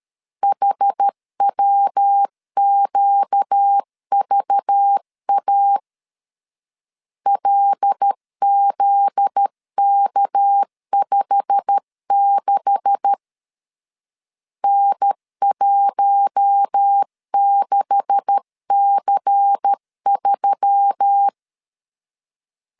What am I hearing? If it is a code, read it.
Morse code, heard as a keyed tone, about 13 wpm, 800 Hz. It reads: HWQVA LZK56 N16C3
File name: LEZIONI / 9 286 12wpm 2305